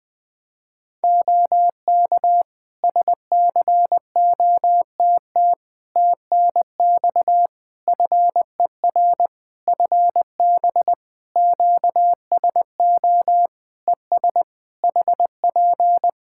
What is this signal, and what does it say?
Morse code, heard as a keyed tone, about 20 wpm, 700 Hz. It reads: OK SCOTT TNX FER FB QSO ES HP
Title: Practice QSO - Exchange 4 - Ending - 20wpm